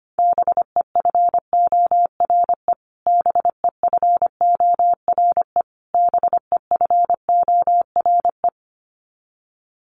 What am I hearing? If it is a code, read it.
Morse code, heard as a keyed tone, about 25 wpm, 700 Hz. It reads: BEFORE BEFORE BEFORE
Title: Top 300 Words Repeated 25wpm